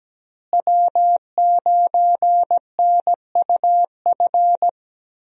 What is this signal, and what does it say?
Morse code, heard as a keyed tone, about 17 wpm, 700 Hz. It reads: W9NUF
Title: USA Call Signs - 5 Characters - Repeated 3x - 17wpm